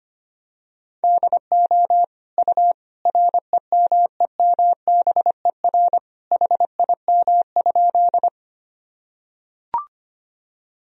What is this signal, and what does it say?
Morse code, heard as a keyed tone, about 25 wpm, 700 Hz. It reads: DO U REMEMBER HIM?
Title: Sentences from Top 400 Words and Abbreviations 25wpm